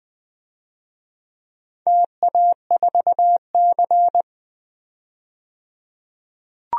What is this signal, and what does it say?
Morse code, heard as a keyed tone, about 20 wpm, 700 Hz. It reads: TA4C
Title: DXCC-100 Call Signs - 4 Characters - 20wpm